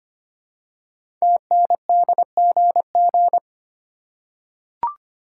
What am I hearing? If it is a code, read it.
Morse code, heard as a keyed tone, about 25 wpm, 690 Hz. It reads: TNDGG
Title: Code Groups - 5 Letters - 25wpm